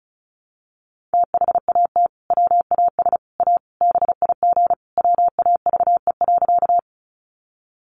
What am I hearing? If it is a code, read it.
Morse code, heard as a keyed tone, about 34 wpm, 700 Hz. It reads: THAT WAS A BIG WAVE.
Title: Sentences from Top 1400 Words 35wpm